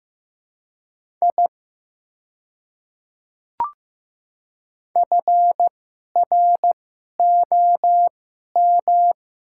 Morse code, heard as a keyed tone, about 15 wpm, spaced as for 11 wpm, 690 Hz. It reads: I FROM
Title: 2x Character Spacing - Top 300 Words - 15wpm